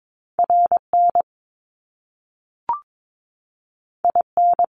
Morse code, heard as a keyed tone, about 22 wpm, 700 Hz. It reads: RN IN
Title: Top Two Letter Combinations in Words 22wpm